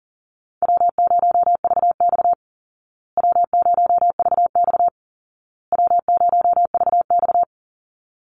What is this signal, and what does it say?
Morse code, heard as a keyed tone, about 40 wpm, 700 Hz. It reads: W0VX W0VX W0VX
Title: Call Signs Repeated 2x - 40wpm